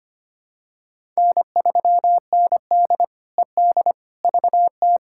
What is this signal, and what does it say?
Morse code, heard as a keyed tone, about 25 wpm, 700 Hz. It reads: N3ND ED VT
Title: K1USN SST QSO Practice - 25wpm